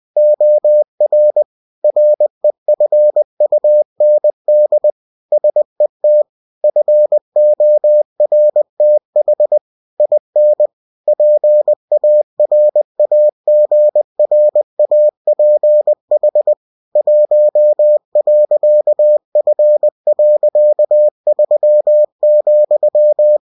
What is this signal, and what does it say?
Morse code, heard as a keyed tone, about 20 wpm, 600 Hz. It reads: OR REFUND SET FORTH IN PARAGRAPH 1.F.3,
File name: cw-books / winnie-the-pooh / winnie-20wpm-page-0119